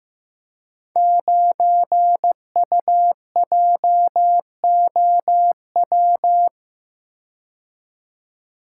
Morse code, heard as a keyed tone, about 15 wpm, 700 Hz. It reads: 9UJOW
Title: Code Groups - 5 Letter:Numbers - 15wpm